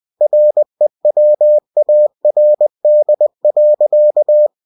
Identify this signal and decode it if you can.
Morse code, heard as a keyed tone, about 20 wpm, 600 Hz. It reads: REWARD.
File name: cw-books / winnie-the-pooh / winnie-20wpm-page-0030